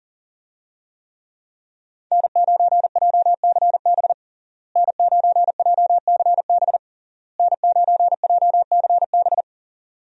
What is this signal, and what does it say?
Morse code, heard as a keyed tone, about 40 wpm, 700 Hz. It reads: N9JCB N9JCB N9JCB
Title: USA Call Signs - 5 Characters - Repeated 3x - 40wpm